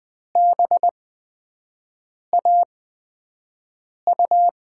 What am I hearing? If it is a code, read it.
Morse code, heard as a keyed tone, about 20 wpm, spaced as for 5 wpm, 700 Hz. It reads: BAU
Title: USA Call Signs - 8x Character Spacing - 20wpm